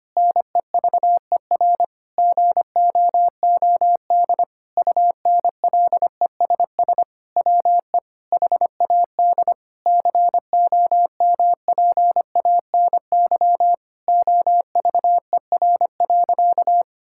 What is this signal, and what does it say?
Morse code, heard as a keyed tone, about 25 wpm, 700 Hz. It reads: NEVER GOOD UNLESS WE HAD COMPANY OVER.
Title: Sentences from Top 1000 Words - Review Part 6 - 25wpm